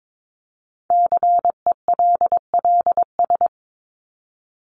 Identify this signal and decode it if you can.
Morse code, heard as a keyed tone, about 22 wpm, 700 Hz. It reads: CELLS
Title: ES:S Suffix - Top 100 Words with Base Words 22wpm